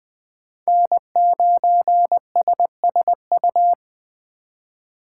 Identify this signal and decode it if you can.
Morse code, heard as a keyed tone, about 23 wpm, 700 Hz. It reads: N9SSU
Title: Speed Racing - USA 1x3 Call Signs 20wpm